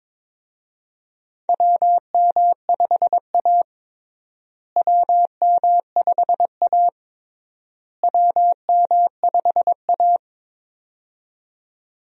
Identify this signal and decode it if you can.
Morse code, heard as a keyed tone, about 22 wpm, 700 Hz. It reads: WM5A WM5A WM5A
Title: USA Call Signs - 4 Characters - Repeated 3x - 22wpm